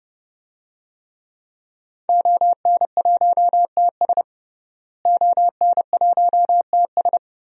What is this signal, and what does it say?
Morse code, heard as a keyed tone, about 29 wpm, 700 Hz. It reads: ON1TS ON1TS
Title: DXCC-100 Call Signs - 5 Characters - Repeated 2x - 30wpm